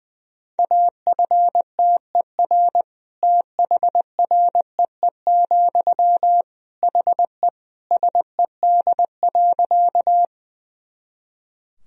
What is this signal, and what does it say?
Morse code, heard as a keyed tone, about 21 wpm, 700 Hz. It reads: AFTER THREE, HE SED.
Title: Sentences from Top 200 Words and Abbreviations 20wpm